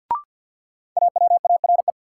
Morse code, heard as a keyed tone, about 53 wpm, 700 Hz. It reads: AWARE
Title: Sentences from Top 1200 Words - New Word Drill - 50wpm